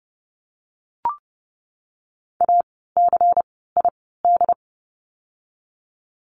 Morse code, heard as a keyed tone, about 30 wpm, spaced as for 17 wpm, 700 Hz. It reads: ACID
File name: Lesson 16f - Letter c - Words 30wpm